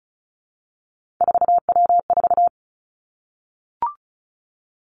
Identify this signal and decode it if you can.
Morse code, heard as a keyed tone, about 35 wpm, 700 Hz. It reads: 4W4
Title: Three Challenging Letter-Number Combinations 35wpm